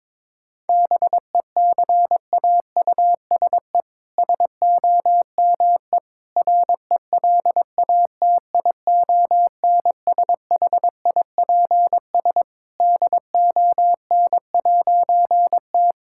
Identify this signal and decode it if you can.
Morse code, heard as a keyed tone, about 21 wpm, 700 Hz. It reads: BECAUSE SOME RELATIONSHIPS DON'T
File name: All-Age Appropriate Jokes - 22wpm